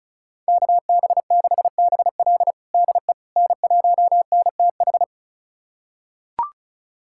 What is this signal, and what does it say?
Morse code, heard as a keyed tone, about 36 wpm, 700 Hz. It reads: KB6BL DE N1NTH
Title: USA Callsign DE Callsign - 5 Characters - 35wpm